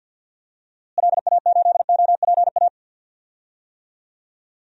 Morse code, heard as a keyed tone, about 48 wpm, 700 Hz. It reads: LA8OPA
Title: DXCC-100 Call Signs - 6 Characters - Repeated 2x - 50wpm